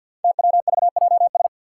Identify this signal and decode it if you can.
Morse code, heard as a keyed tone, about 47 wpm, 700 Hz. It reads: TWVJS
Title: Five Challenging Letter-Number Combinations 50wpm